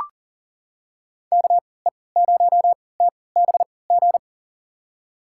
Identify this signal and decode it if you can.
Morse code, heard as a keyed tone, about 40 wpm, spaced as for 23 wpm, 700 Hz. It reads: KE0TBG
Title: USA Call Signs - 6 Characters - 3x Character Spacing - 40wpm